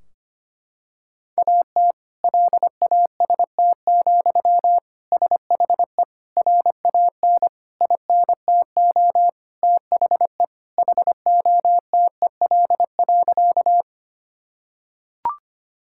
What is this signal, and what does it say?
Morse code, heard as a keyed tone, about 25 wpm, 700 Hz. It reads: AT LAST, SHE RAN INTO THE HOTEL.
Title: Sentences from Top 1600 Words 25wpm